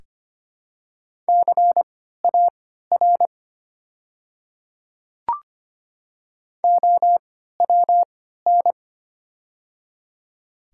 Morse code, heard as a keyed tone, about 25 wpm, spaced as for 14 wpm, 700 Hz. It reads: CAR OWN
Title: 3x Character Spacing - Top 500 Words - 25wpm